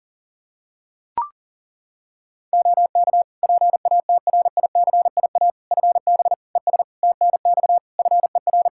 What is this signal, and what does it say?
Morse code, heard as a keyed tone, about 40 wpm, 700 Hz. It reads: OK PATRICIA FB ES TNX FER
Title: Practice QSO - Exchange 2 - Info - 40wpm